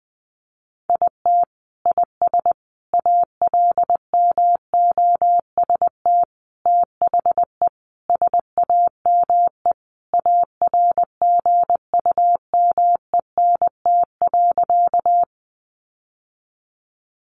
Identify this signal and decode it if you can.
Morse code, heard as a keyed tone, about 20 wpm, 700 Hz. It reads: IT IS ALMOST THE SAME ARGUMENT.
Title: Sentences from Top 1400 Words 20wpm